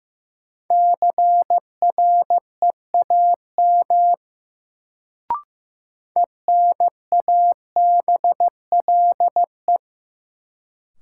Morse code, heard as a keyed tone, about 15 wpm, 700 Hz. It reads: CREAM ENABLE
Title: Sentences from Top 1500 Words - New Word Drill - 15wpm